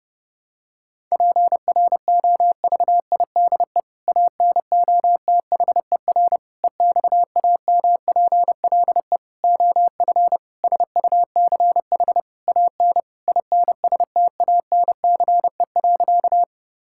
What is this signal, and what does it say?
Morse code, heard as a keyed tone, about 30 wpm, 700 Hz. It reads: PROVIDE ANOTHER EXAMPLE OF SUCH AN INSTANCE.
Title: Sentences from Top 1000 Words - Review Part 5 - 30wpm